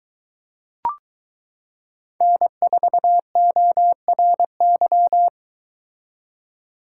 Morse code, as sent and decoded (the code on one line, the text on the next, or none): -. ....- --- .-. -.--
N4ORY